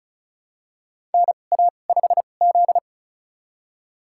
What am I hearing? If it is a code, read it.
Morse code, heard as a keyed tone, about 36 wpm, spaced as for 25 wpm, 700 Hz. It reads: NA5Z
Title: USA Call Signs - 2x Character Spacing - 35wpm